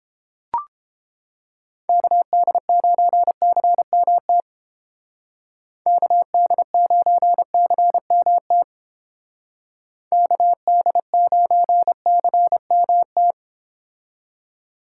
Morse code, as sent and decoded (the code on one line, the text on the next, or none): -.- -.. ----. -.-. -- - / -.- -.. ----. -.-. -- - / -.- -.. ----. -.-. -- -
KD9CMT KD9CMT KD9CMT